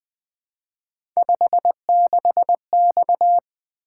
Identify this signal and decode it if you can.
Morse code, heard as a keyed tone, about 20 wpm, 700 Hz. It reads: 56X
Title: Three Letter-Number Combinations - 20wpm